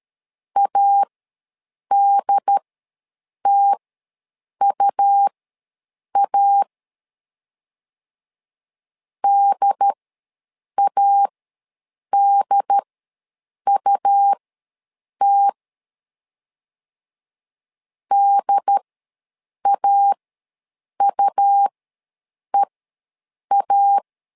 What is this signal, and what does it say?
Morse code, heard as a keyed tone, about 13 wpm, spaced as for 7 wpm, 800 Hz. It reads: ADTUA DADUT DAUEA
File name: LEZIONI / 2 UHDN / 2 UHD ws 12wpm 1722